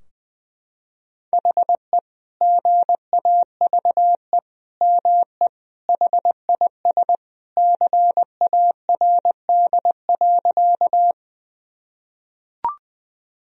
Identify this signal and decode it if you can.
Morse code, heard as a keyed tone, about 21 wpm, 700 Hz. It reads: HE GAVE ME HIS CARD.
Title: Sentences from Top 1000 Words - Review Part 4 - 20wpm